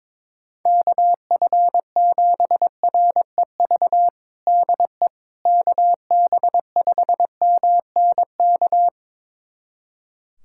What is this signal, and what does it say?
Morse code, heard as a keyed tone, about 23 wpm, 700 Hz. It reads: KF7REV DE KB5MNK
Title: USA Callsign DE Callsign - 6 Characters - 22wpm